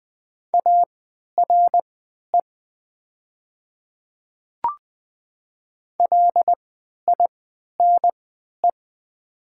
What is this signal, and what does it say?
Morse code, heard as a keyed tone, about 20 wpm, spaced as for 11 wpm, 700 Hz. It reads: ARE LINE
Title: 3x Character Spacing - Top 200 Words - 20wpm